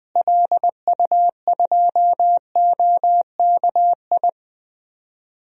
Morse code, heard as a keyed tone, about 21 wpm, 700 Hz. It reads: LU2OKI
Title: DXCC-100 Call Signs - 6 Characters - Repeated 3x - 20wpm